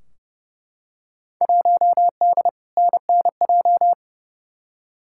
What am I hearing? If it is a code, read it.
Morse code, heard as a keyed tone, about 30 wpm, 700 Hz. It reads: 1D NNJ
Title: Field Day Exchanges - 30wpm